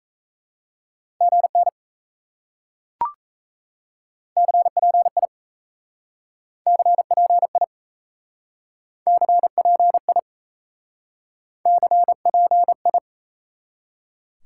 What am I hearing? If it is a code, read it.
Morse code, heard as a keyed tone, about 37 wpm, 700 Hz. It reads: GN CPI CPI CPI CPI